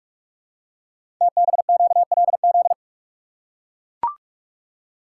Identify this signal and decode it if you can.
Morse code, heard as a keyed tone, about 43 wpm, 700 Hz. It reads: TBQLZ
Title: Code Groups - 5 Letter:Numbers - 45wpm